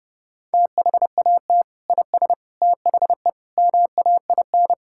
Morse code, heard as a keyed tone, about 32 wpm, 700 Hz. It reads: THAT IS THE MAIN
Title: Sentences from Top 1000 Words - Review Part 5 - 30wpm